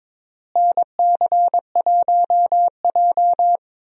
Morse code, heard as a keyed tone, about 21 wpm, 700 Hz. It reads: NC1J
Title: USA Call Signs - 4 Characters - Repeated 3x - 22wpm